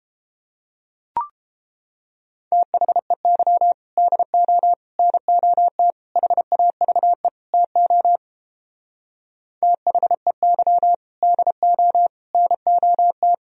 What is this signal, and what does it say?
Morse code, heard as a keyed tone, about 29 wpm, 700 Hz. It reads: THEY DO NOT HAVE TO THEY DO NOT